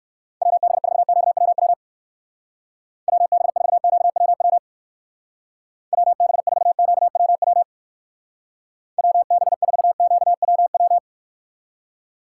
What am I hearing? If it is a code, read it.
Morse code, heard as a keyed tone, about 59 wpm, 700 Hz. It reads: WB4QWW WB4QWW WB4QWW WB4QWW